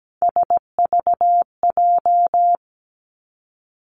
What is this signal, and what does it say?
Morse code, heard as a keyed tone, about 17 wpm, 700 Hz. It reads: SVJ
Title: Three Challenging Letter-Number Combinations 17wpm